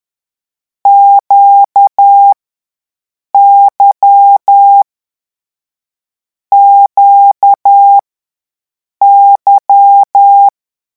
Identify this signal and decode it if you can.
Morse code, heard as a keyed tone, about 10 wpm, 800 Hz. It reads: Q Y Q Y
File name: morse mp3 / lecon2-vitesse 10